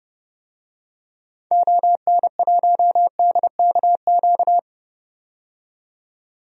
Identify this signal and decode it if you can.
Morse code, heard as a keyed tone, about 30 wpm, 690 Hz. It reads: ON1DKQ
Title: DXCC-100 Call Signs - 6 Characters - 30wpm